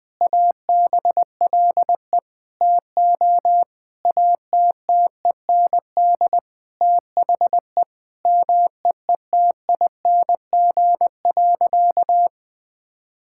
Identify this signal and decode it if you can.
Morse code, heard as a keyed tone, about 19 wpm, 700 Hz. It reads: ABLE TO ATTEND THE MEETING.
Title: Sentences with 5-Word Phrases - 20wpm